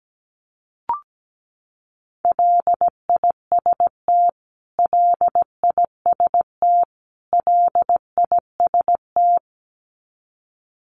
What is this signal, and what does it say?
Morse code, heard as a keyed tone, about 17 wpm, 700 Hz. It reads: LIST LIST LIST
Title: Top 500 Words Repeated 17wpm